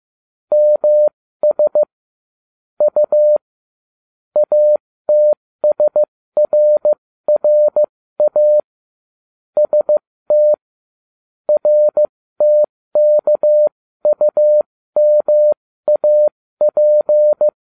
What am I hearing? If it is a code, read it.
Morse code, heard as a keyed tone, about 15 wpm, spaced as for 13 wpm, 600 Hz. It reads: MS U ATSRRA ST RTKUMAP
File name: cw / ubacw07 1